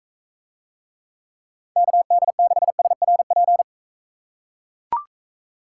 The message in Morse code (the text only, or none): KD6SRP